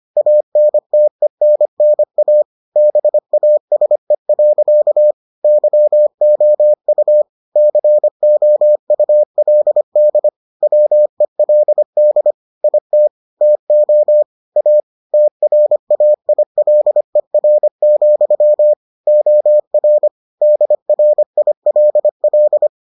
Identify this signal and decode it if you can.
Morse code, heard as a keyed tone, about 25 wpm, 600 Hz. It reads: ANTENNA BASE. YOU COULD WELD IT TO A TRAILER, OR DRILL